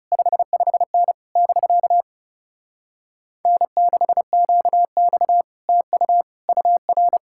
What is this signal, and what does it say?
Morse code, heard as a keyed tone, about 40 wpm, 700 Hz. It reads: 55N <BK> N6QX TU UR